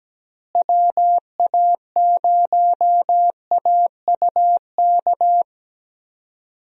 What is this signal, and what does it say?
Morse code, heard as a keyed tone, about 17 wpm, 700 Hz. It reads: WA0AUK
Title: USA Call Signs - 6 Characters - Repeated 2x - 17wpm